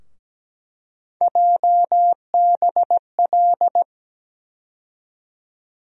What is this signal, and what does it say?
Morse code, heard as a keyed tone, about 17 wpm, 700 Hz. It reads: JBL
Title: USA Call Sign Postfixes - 3 Letters 17wpm